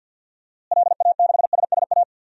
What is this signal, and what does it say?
Morse code, heard as a keyed tone, about 53 wpm, 700 Hz. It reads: LA6SSA
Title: DXCC-100 Call Signs - 6 Characters - Repeated 2x - 50wpm